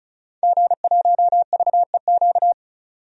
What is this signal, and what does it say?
Morse code, heard as a keyed tone, about 36 wpm, 700 Hz. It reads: G1VEQ
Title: DXCC-100 Call Signs - 5 Characters - 35wpm